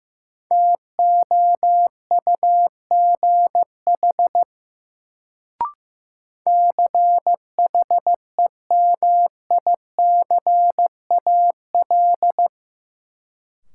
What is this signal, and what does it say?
Morse code, heard as a keyed tone, about 15 wpm, 700 Hz. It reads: TOUGH CHEMICAL
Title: Sentences from Top 1200 Words - New Word Drill Alt - 15wpm